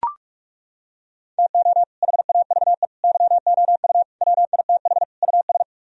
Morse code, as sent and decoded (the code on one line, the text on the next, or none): - --- / .... .- ...- . / -.-- --- ..- / .-- .. - .... / ..- ...
TO HAVE YOU WITH US